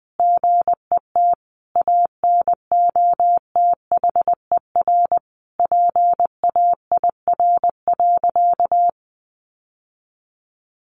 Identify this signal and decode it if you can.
Morse code, heard as a keyed tone, about 20 wpm, 700 Hz. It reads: GET ANOTHER PAIR.
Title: Sentences from Top 1200 Words 20wpm